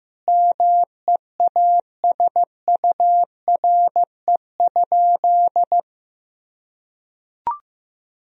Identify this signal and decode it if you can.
Morse code, heard as a keyed tone, about 15 wpm, 700 Hz. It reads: MEASURE?
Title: Sentences from Top 1000 Words - Review Part 2 - 15wpm